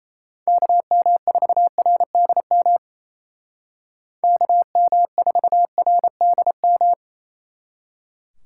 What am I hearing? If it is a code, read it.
Morse code, heard as a keyed tone, about 33 wpm, 700 Hz. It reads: KM4RDM KM4RDM
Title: Speed Racing - USA 2x3 Call Signs 28wpm